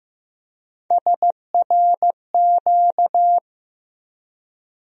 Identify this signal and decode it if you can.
Morse code, heard as a keyed tone, about 15 wpm, 700 Hz. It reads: SRQ
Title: USA Call Sign Postfixes - 3 Letters 15wpm